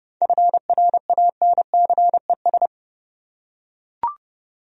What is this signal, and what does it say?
Morse code, heard as a keyed tone, about 32 wpm, 700 Hz. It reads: FRANCES
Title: USA Common Names - 30wpm